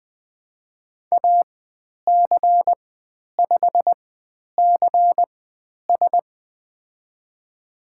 Morse code, heard as a keyed tone, about 20 wpm, spaced as for 10 wpm, 700 Hz. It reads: AC5CS
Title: 095 Lesson 16g - Letter c - Call Signs 20wpm